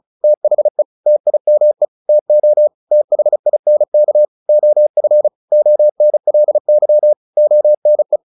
Morse code, heard as a keyed tone, about 34 wpm, 600 Hz. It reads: THE TIME TO THINK OF ONLY ONE
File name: the old man and the sea / the old man and the sea 1 35wpm